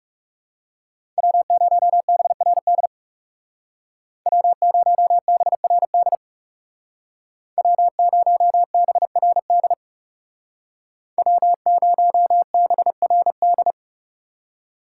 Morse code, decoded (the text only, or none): W0BRD W0BRD W0BRD W0BRD